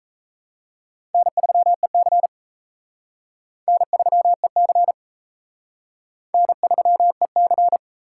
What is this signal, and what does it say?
Morse code, heard as a keyed tone, about 37 wpm, 700 Hz. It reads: N3EC N3EC N3EC